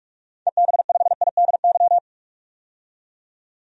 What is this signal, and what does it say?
Morse code, heard as a keyed tone, about 44 wpm, 700 Hz. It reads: EB5IDY
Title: DXCC-100 Call Signs - 6 Characters - 45wpm